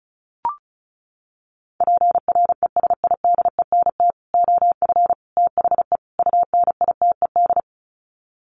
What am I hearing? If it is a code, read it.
Morse code, heard as a keyed tone, about 34 wpm, 700 Hz. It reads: PRESIDENT OF THE UNITED
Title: Sets of 4 Words 35wpm